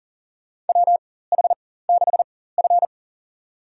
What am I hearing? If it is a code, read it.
Morse code, heard as a keyed tone, about 40 wpm, spaced as for 19 wpm, 700 Hz. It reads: WH6F
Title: USA Call Signs - 4x Character Spacing - 40wpm